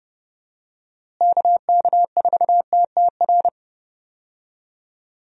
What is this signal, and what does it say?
Morse code, heard as a keyed tone, about 32 wpm, 700 Hz. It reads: KK4TTR
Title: USA Call Signs - 6 Characters - Repeated 2x - 30wpm